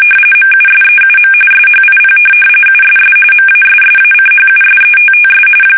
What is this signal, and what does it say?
RTTY, about 45.45 baud. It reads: MUSEUM SHIP USS MASSACHUSETTS
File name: fsk850 / USS-Massachusetts